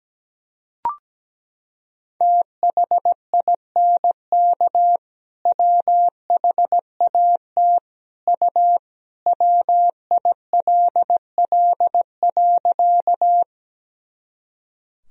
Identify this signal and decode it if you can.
Morse code, heard as a keyed tone, about 17 wpm, 700 Hz. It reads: THINK WHAT U WILL.
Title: Sentences from Top 200 Words and Abbreviations 17wpm